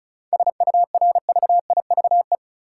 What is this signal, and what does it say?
Morse code, heard as a keyed tone, about 36 wpm, 700 Hz. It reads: SURVIVE
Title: Sentences from Top 1300 Words - New Word Drill - 35wpm